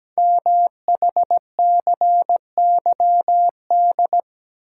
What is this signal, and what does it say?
Morse code, heard as a keyed tone, about 17 wpm, 700 Hz. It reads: MHCYD
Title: Code Groups - 5 Letters - 17wpm